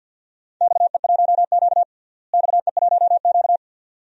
Morse code, heard as a keyed tone, about 52 wpm, 700 Hz. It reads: XE1Q XE1Q
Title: DXCC-100 Call Signs - 4 Characters - Repeated 2x - 50wpm